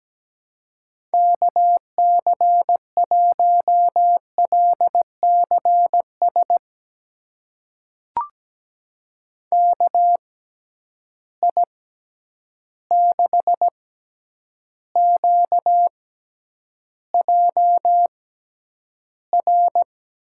Morse code, heard as a keyed tone, about 17 wpm, spaced as for 9 wpm, 700 Hz. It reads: KC1LCS KI6QJR
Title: USA Call Signs - 6 Characters - 6x Character Spacing - 17wpm